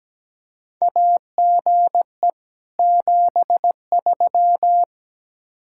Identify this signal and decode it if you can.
Morse code, heard as a keyed tone, about 17 wpm, 700 Hz. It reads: AGE 73
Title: Practice QSO - Exchange 3 - More Info - 17wpm